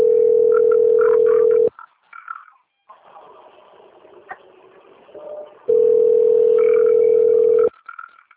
Ringback tone (North America), 440 + 480 Hz.
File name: rg-1600-unknown- - - .27741